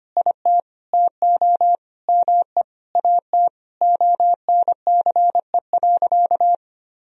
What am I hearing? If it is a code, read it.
Morse code, heard as a keyed tone, about 25 wpm, 700 Hz. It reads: IT TO ME AT ONCE.
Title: Sentences from Top 500 Words and Abbreviations 25wpm